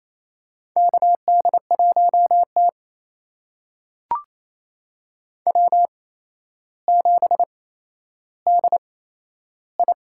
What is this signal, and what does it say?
Morse code, heard as a keyed tone, about 28 wpm, spaced as for 12 wpm, 700 Hz. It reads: KD1T W7DI
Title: USA Call Signs - 4 Characters - 8x Character Spacing - 28wpm